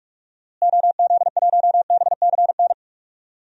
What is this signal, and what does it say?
Morse code, heard as a keyed tone, about 41 wpm, 700 Hz. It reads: OZ1BCN
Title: DXCC-100 Call Signs - 6 Characters - Repeated 3x - 45wpm